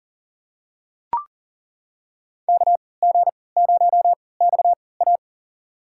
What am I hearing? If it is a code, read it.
Morse code, heard as a keyed tone, about 40 wpm, spaced as for 23 wpm, 700 Hz. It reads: KG0XA
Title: USA Call Signs - 5 Characters - 3x Character Spacing - 40wpm